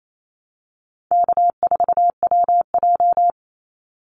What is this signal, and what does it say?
Morse code, heard as a keyed tone, about 28 wpm, 700 Hz. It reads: K4WJ
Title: Easy Call Signs Repeated 3x - 28wpm